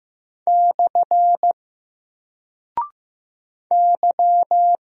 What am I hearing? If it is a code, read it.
Morse code, heard as a keyed tone, about 15 wpm, 700 Hz. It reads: / Y
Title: Single Character (Letter:Number:Punctuation:Prosign) 15wpm